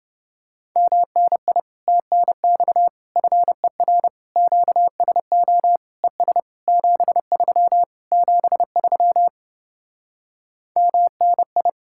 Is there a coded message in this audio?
Yes: Morse code from 0.8 to 11.7 s, heard as a keyed tone, about 28 wpm, 700 Hz. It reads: MNI TNX FER QSO ES 73 73 MNI